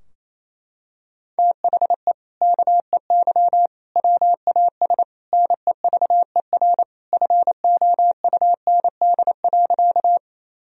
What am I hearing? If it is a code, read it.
Morse code, heard as a keyed tone, about 28 wpm, 700 Hz. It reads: THE KEY WAS NEVER FOUND.
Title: Sentences from Top 1000 Words - Review Part 4 - 28wpm